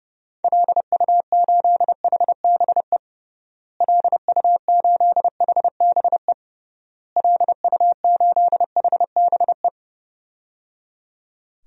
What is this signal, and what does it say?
Morse code, heard as a keyed tone, about 31 wpm, 700 Hz. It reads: LU8HBE LU8HBE LU8HBE
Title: DXCC-100 Call Signs - 6 Characters - Repeated 3x - 30wpm